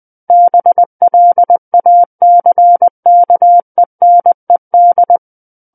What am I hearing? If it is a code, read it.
Morse code, heard as a keyed tone, about 20 wpm, 700 Hz. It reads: BLACKENED
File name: 20 WPM